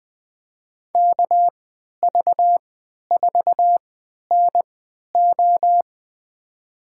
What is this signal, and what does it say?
Morse code, heard as a keyed tone, about 20 wpm, spaced as for 11 wpm, 700 Hz. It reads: KV4NO
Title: USA Call Signs - 5 Characters - 3x Character Spacing - 20wpm